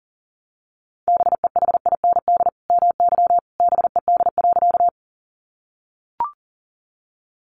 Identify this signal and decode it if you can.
Morse code, heard as a keyed tone, about 40 wpm, 700 Hz. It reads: BEHIND MY BED.
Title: Sentences from Top 500 Words 40wpm